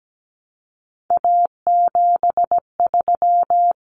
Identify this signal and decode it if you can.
Morse code, heard as a keyed tone, about 17 wpm, 700 Hz. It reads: A73
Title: Common Cut Numbers - 3-Digit 17wpm